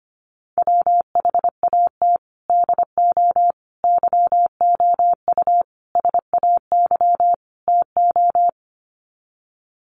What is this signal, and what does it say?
Morse code, heard as a keyed tone, about 25 wpm, 700 Hz. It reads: WHAT DO YOU SAY TO
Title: Sets of 5 Words 25wpm